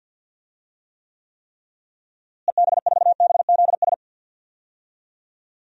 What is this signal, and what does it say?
Morse code, heard as a keyed tone, about 46 wpm, 700 Hz. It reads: EB4BZS
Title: DXCC-100 Call Signs - 6 Characters - Repeated 3x - 50wpm